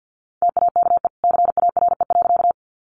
Morse code, heard as a keyed tone, about 50 wpm, 700 Hz. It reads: TAKE CARE.
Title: Sentences from Top 300 Words 50wpm